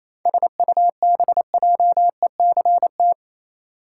Morse code, heard as a keyed tone, about 28 wpm, 700 Hz. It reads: SUBJECT
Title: Sentences from Top 800 Words - New Word Drill - 28wpm